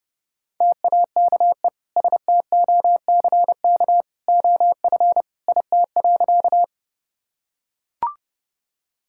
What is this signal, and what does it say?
Morse code, heard as a keyed tone, about 30 wpm, 700 Hz. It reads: TAKE STOCK OF IT.
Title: Sentences from Top 1000 Words - Review Part 5 - 30wpm